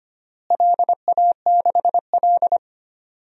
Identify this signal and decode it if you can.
Morse code, heard as a keyed tone, about 25 wpm, 700 Hz. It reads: LA6L
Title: DXCC-100 Call Signs - 4 Characters - Repeated 3x - 25wpm